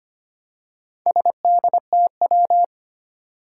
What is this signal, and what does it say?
Morse code, heard as a keyed tone, about 25 wpm, 700 Hz. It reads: SDTW
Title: Four Challenging Letter-Number Combinations 25wpm